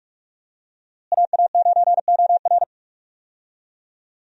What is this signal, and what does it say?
Morse code, heard as a keyed tone, about 46 wpm, 700 Hz. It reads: AA9OR
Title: USA Call Signs - 5 Characters - Repeated 3x - 45wpm